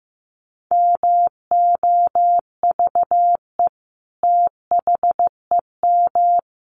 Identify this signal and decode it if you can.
Morse code, heard as a keyed tone, about 15 wpm, 700 Hz. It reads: MOVE THEM
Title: Sentences from Top 1400 Words 15wpm